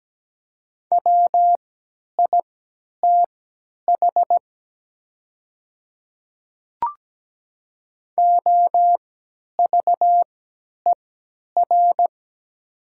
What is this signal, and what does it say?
Morse code, heard as a keyed tone, about 17 wpm, spaced as for 10 wpm, 700 Hz. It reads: WITH OVER
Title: 3x Character Spacing - Top 100 Words - 17wpm